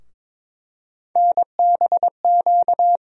Morse code, heard as a keyed tone, about 22 wpm, 700 Hz. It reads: NBQ
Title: USA Call Sign Postfixes - 3 Letters 22wpm